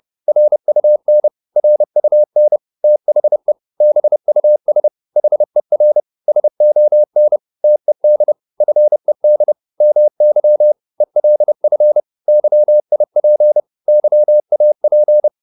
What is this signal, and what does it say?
Morse code, heard as a keyed tone, about 30 wpm, 600 Hz. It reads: RUN RUN THE BUS HER SON TED FED MY ELF YIP YAP